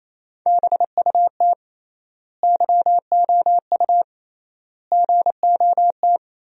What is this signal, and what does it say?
Morse code, heard as a keyed tone, about 28 wpm, 700 Hz. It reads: BUT YOU GOT
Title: 3x Word-Spacing - Sentences from Top 300 Words - 28wpm